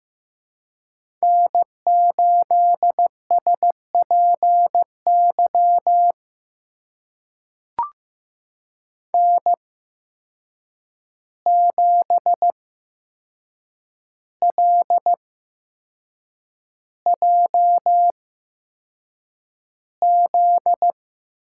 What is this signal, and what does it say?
Morse code, heard as a keyed tone, about 15 wpm, spaced as for 6 wpm, 700 Hz. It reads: N8SPY N7LJZ
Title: USA Call Signs - 5 Characters - 8x Character Spacing - 15wpm